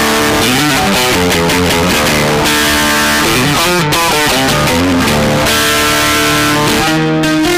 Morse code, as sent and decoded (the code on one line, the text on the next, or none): none